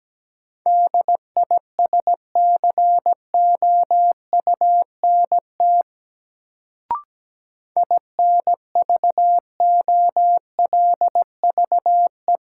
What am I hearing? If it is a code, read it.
Morse code, heard as a keyed tone, about 17 wpm, 700 Hz. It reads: DISCOUNT INVOLVE
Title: Sentences from Top 1200 Words - New Word Drill - 17wpm